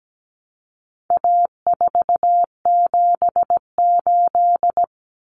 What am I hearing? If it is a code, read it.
Morse code, heard as a keyed tone, about 17 wpm, 700 Hz. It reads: A478
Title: Common Cut Numbers - 4-Digit 17wpm